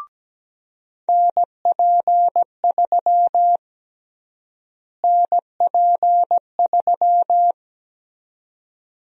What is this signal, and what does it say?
Morse code, heard as a keyed tone, about 17 wpm, 700 Hz. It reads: NP3 NP3
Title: USA Call Sign Prefixes - Repeated 2x - 17wpm